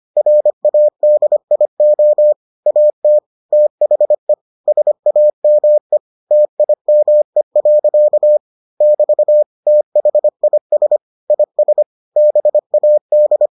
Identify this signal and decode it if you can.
Morse code, heard as a keyed tone, about 25 wpm, 600 Hz. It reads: RADIO AT THE SAME TIME. <BT> THIS IS BAD